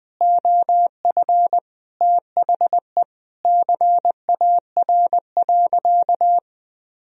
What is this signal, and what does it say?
Morse code, heard as a keyed tone, about 20 wpm, 700 Hz. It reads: OF THE CAR.
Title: Sentences from Top 1000 Words - Review Part 1 - 20wpm